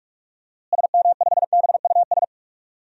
Morse code, heard as a keyed tone, about 45 wpm, 700 Hz. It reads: SM5BUS
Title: DXCC-100 Call Signs - 6 Characters - Repeated 2x - 45wpm